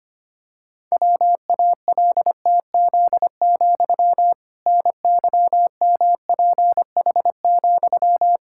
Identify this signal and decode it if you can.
Morse code, heard as a keyed tone, about 25 wpm, 700 Hz. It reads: WALTZ, NYMPH,
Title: Pangram Challenge v2 - 25wpm